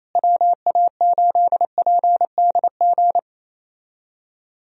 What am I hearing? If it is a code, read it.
Morse code, heard as a keyed tone, about 28 wpm, 700 Hz. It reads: WA8PDG
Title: USA Call Signs - 6 Characters - Repeated 3x - 28wpm